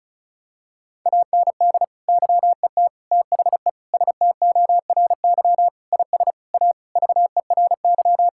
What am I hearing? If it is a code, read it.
Morse code, heard as a keyed tone, about 34 wpm, 700 Hz. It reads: AND YET THE STORY IS A VERY
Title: Sentences from Top 1000 Words - Review Part 6 - 35wpm